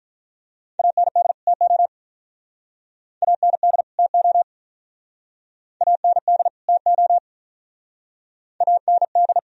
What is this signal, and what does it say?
Morse code, heard as a keyed tone, about 47 wpm, 700 Hz. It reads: AND TO AND TO AND TO AND